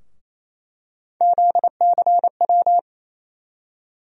Morse code, heard as a keyed tone, about 28 wpm, 700 Hz. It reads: ZCW
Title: USA Call Sign Postfixes - 3 Letters 28wpm